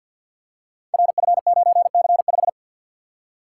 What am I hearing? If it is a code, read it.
Morse code, heard as a keyed tone, about 48 wpm, 700 Hz. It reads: RF9C5
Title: Code Groups - 5 Letter:Numbers - 50wpm